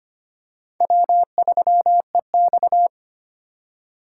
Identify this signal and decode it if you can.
Morse code, heard as a keyed tone, about 25 wpm, 700 Hz. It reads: W3EX
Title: USA Call Signs - 4 Characters - 4x Character Spacing - 25wpm